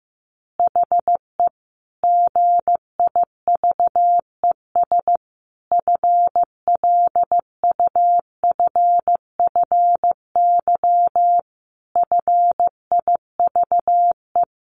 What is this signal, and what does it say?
Morse code, heard as a keyed tone, about 15 wpm, 700 Hz. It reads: HE GIVES FLUFFY FIVE
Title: Easy Phrases 15wpm